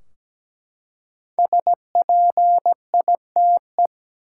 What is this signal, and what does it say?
Morse code, heard as a keyed tone, about 17 wpm, 700 Hz. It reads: SPITE
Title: Five Letter Words - 17wpm